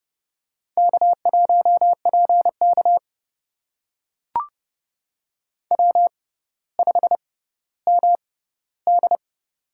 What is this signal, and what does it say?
Morse code, heard as a keyed tone, about 28 wpm, spaced as for 15 wpm, 690 Hz. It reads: K1PK W5MD
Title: USA Call Signs - 4 Characters - 6x Character Spacing - 30wpm